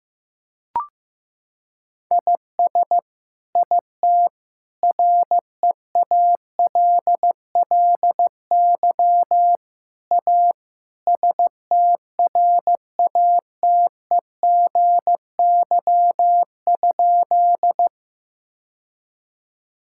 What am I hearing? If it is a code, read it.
Morse code, heard as a keyed tone, about 15 wpm, 700 Hz. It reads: IS IT REALLY A STRATEGY?
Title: Sentences from Top 1000 Words - Review Part 5 - 15wpm